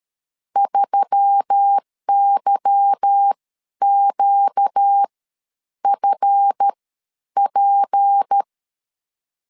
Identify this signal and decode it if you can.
Morse code, heard as a keyed tone, about 13 wpm, spaced as for 9 wpm, 800 Hz. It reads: 3YQFP
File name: LEZIONI / 8 YJ45 / 8 Y 12wpm 2228